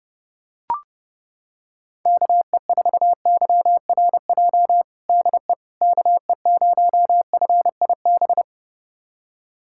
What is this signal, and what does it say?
Morse code, heard as a keyed tone, about 28 wpm, 700 Hz. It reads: KE4YRJ DE KE0FIB
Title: USA Callsign DE Callsign - 6 Characters - 30wpm